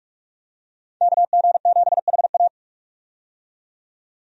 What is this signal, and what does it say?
Morse code, heard as a keyed tone, about 44 wpm, 700 Hz. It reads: KG7HA
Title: USA Call Signs - 5 Characters - 45wpm